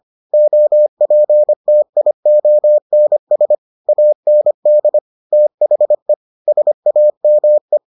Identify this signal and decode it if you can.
Morse code, heard as a keyed tone, about 25 wpm, 600 Hz. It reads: OPTIONS AND THE SAME